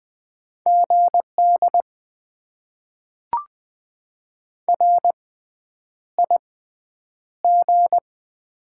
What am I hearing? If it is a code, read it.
Morse code, heard as a keyed tone, about 20 wpm, spaced as for 9 wpm, 700 Hz. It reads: GD RIG
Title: Top 90 QSO Elements - 6x Character Spacing - 20wpm